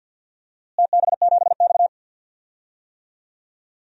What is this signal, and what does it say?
Morse code, heard as a keyed tone, about 50 wpm, 700 Hz. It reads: TB7X
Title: DXCC-100 Call Signs - 4 Characters - Repeated 3x - 50wpm